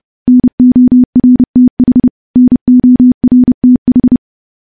Morse code, heard as a keyed tone, about 30 wpm, 250 Hz. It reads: NORTH NORTH
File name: morse / morse-training-30WPM